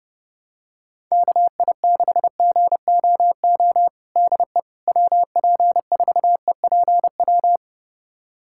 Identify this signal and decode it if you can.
Morse code, heard as a keyed tone, about 30 wpm, 700 Hz. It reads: KI6GOO DE WP4EPW
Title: USA Callsign DE Callsign - 6 Characters - 30wpm